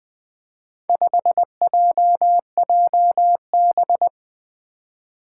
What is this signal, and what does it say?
Morse code, heard as a keyed tone, about 20 wpm, 700 Hz. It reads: 5JJB
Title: Four Challenging Letter-Number Combinations 20wpm